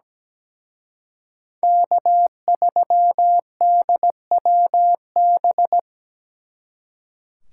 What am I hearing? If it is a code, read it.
Morse code, heard as a keyed tone, about 20 wpm, 700 Hz. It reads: K3DWB
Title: Speed Racing - USA 1x3 Call Signs 17wpm